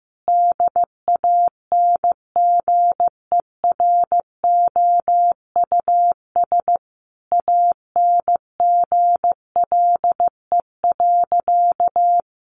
Morse code, heard as a keyed tone, about 15 wpm, 700 Hz. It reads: DANGEROUS ANGLE.
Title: Sentences from Top 1500 Words 15wpm